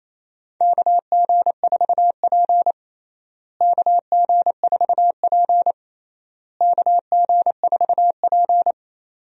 Morse code, heard as a keyed tone, about 28 wpm, 700 Hz. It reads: KG4P KG4P KG4P
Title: USA Call Signs - 4 Characters - Repeated 3x - 28wpm